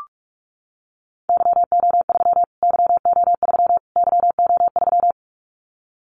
Morse code, heard as a keyed tone, about 44 wpm, 700 Hz. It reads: YO3 YO3 YO3
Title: Callsign Prefixes for Top 25 DXCC - Repeated 3x - 45wpm